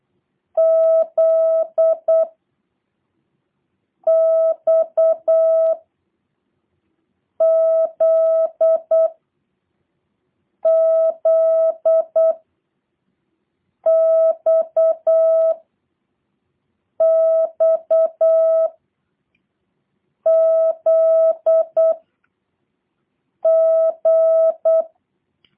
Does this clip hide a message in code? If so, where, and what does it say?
Yes: Morse code from 0.6 to 24.8 s, heard as a keyed tone, about 8 wpm, 650 Hz. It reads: Z X Z Z X X Z G